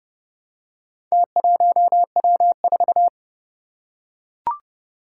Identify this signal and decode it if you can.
Morse code, heard as a keyed tone, about 31 wpm, 700 Hz. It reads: T1W4
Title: Four Challenging Letter-Number Combinations 30wpm